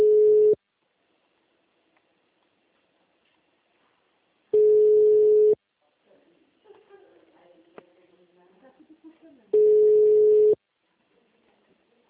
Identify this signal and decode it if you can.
Ringback tone (Europe, 425 Hz), with speech.
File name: out- -2001- - - .11317